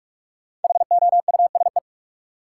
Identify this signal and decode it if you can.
Morse code, heard as a keyed tone, about 42 wpm, 700 Hz. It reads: HOUSE
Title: Sentences from Top 200 Words - New Word Drill - 45wpm